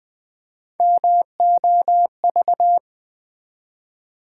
Morse code, heard as a keyed tone, about 20 wpm, 700 Hz. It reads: MOV
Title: Build A Word - 3 Letters At A Time - 8-Letter Words - 20wpm